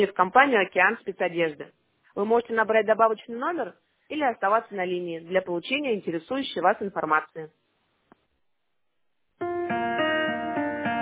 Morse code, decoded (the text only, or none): none